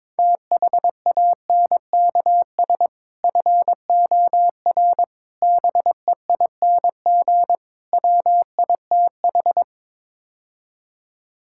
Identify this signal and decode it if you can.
Morse code, heard as a keyed tone, about 22 wpm, 700 Hz. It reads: THANKS FOR BEING WITH
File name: practice / 22wpm / Build A Phrase - 5-Word Phrases - 22wpm